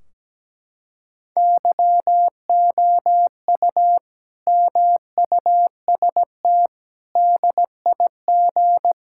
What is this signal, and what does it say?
Morse code, heard as a keyed tone, about 17 wpm, 700 Hz. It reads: YOU MUST DIG
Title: Sentences from Top 1600 Words 17wpm